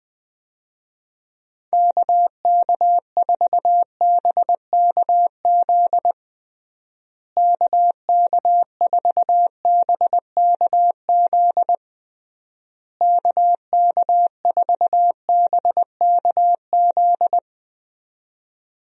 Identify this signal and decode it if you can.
Morse code, heard as a keyed tone, about 20 wpm, 700 Hz. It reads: KK4BKZ KK4BKZ KK4BKZ
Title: USA Call Signs - 6 Characters - Repeated 3x - 20wpm